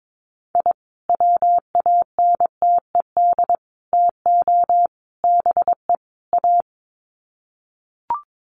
Morse code, heard as a keyed tone, about 22 wpm, 700 Hz. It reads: I WANTED TO BE A
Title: Sets of 5 Words Encore2 22wpm